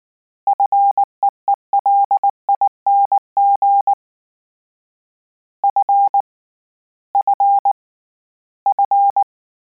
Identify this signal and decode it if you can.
Morse code, heard as a keyed tone, about 19 wpm, 800 Hz. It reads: FEELING F F F
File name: Combinations / Combinations-mp3s / Combinations@18